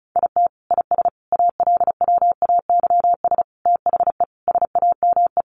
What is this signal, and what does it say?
Morse code, heard as a keyed tone, about 34 wpm, 700 Hz. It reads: IT IS ALWAYS THE SAME
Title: Sentences from Top 1400 Words 35wpm